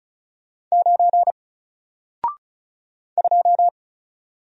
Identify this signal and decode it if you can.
Morse code, heard as a keyed tone, about 35 wpm, 700 Hz. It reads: 9 2
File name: Single-Letter-Number-35wpm